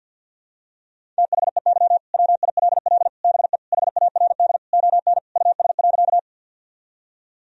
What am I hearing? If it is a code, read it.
Morse code, heard as a keyed tone, about 52 wpm, 700 Hz. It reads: THEY WILL BE HARD ON US.
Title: Sentences from Top 1000 Words - Review Part 1 - 50wpm